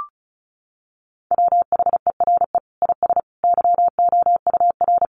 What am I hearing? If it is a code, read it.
Morse code, heard as a keyed tone, about 34 wpm, 700 Hz. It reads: WHERE IS YOUR
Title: Sentences from Top 1400 Words 35wpm